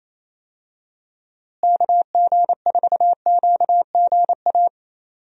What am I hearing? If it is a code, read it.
Morse code, heard as a keyed tone, about 28 wpm, 700 Hz. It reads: KG4QGA
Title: USA Call Signs - 6 Characters - Repeated 3x - 28wpm